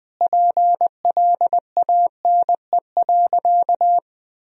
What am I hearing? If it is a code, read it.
Morse code, heard as a keyed tone, about 20 wpm, 700 Hz. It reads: PLANE.
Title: Sentences from Top 1000 Words - Review Part 3 - 20wpm